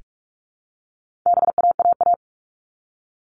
Morse code, heard as a keyed tone, about 44 wpm, 700 Hz. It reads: BAAA
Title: All Cut Numbers - 4-Digit 45wpm